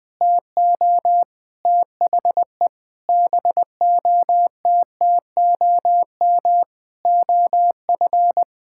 Morse code, heard as a keyed tone, about 20 wpm, 700 Hz. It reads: TO THE BOTTOM OF
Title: Sentences with 6-Word Phrases - 20wpm